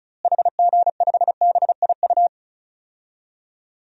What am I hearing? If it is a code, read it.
Morse code, heard as a keyed tone, about 34 wpm, 700 Hz. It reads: HG5BIU
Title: DXCC-100 Call Signs - 6 Characters - Repeated 3x - 35wpm